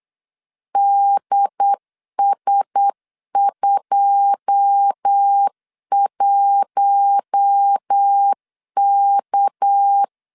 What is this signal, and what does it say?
Morse code, heard as a keyed tone, about 8 wpm, 800 Hz. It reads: DS21K
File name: LEZIONI / 9 28 8wpm 3512